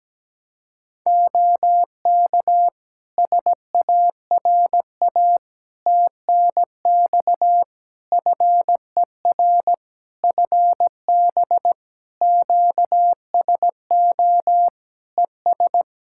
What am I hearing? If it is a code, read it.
Morse code, heard as a keyed tone, about 17 wpm, 700 Hz. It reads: OK SARA TNX FER FB QSO ES
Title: Practice QSO - Exchange 4 - Ending - 17wpm